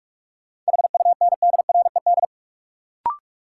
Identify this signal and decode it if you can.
Morse code, heard as a keyed tone, about 46 wpm, 700 Hz. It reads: HUNDRED
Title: Sentences from Top 400 Words - New Word Drill - 45wpm